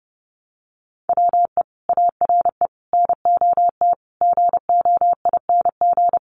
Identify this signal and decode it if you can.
Morse code, heard as a keyed tone, about 30 wpm, 700 Hz. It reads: WE ARE NOT GOING
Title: Sets of 5 Words 30wpm